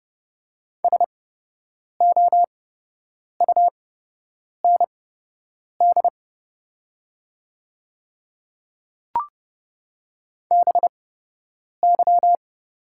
Morse code, heard as a keyed tone, about 30 wpm, spaced as for 8 wpm, 700 Hz. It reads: SOUND BY
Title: 8x Character Spacing - Top 100 Words - 30wpm